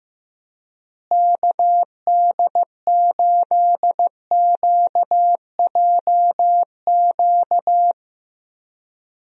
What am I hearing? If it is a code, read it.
Morse code, heard as a keyed tone, about 15 wpm, 700 Hz. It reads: KD8QJQ
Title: USA Call Signs - 6 Characters - Repeated 2x - 15wpm